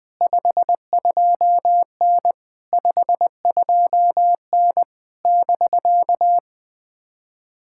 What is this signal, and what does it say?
Morse code, heard as a keyed tone, about 19 wpm, 700 Hz. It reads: 52N 52N <BK>
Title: POTA QSO Practice - 20wpm